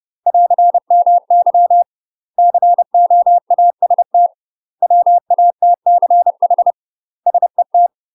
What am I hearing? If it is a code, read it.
Morse code, heard as a keyed tone, about 30 wpm, 700 Hz. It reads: <AR> MY COAST WATCH SET
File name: UK MASTER QSOs TRAINING / 30WPM / UK MASTER QSOs 0019